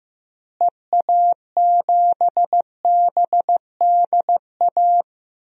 Morse code, heard as a keyed tone, about 15 wpm, 700 Hz. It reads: EA7BDA
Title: DXCC-100 Call Signs - 6 Characters - Repeated 2x - 15wpm